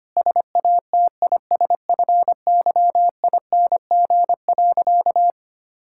Morse code, heard as a keyed tone, about 25 wpm, 700 Hz. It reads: SATISFYING.
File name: practice / 25wpm / Sentences with 3-Word Phrases - 25wpm